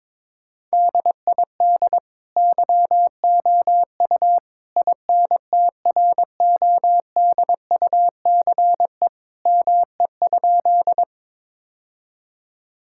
Morse code, heard as a keyed tone, about 21 wpm, 700 Hz. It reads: DID YOU INTRODUCE ME?
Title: Sentences from Top 1600 Words 22wpm